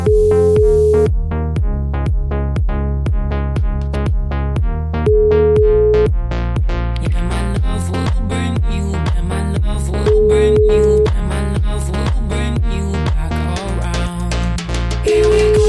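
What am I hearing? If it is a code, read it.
Ringback tone (Europe, 425 Hz), with noise.
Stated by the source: O2 Knihovna welcome-melody service: a tune the caller hears before the call is answered.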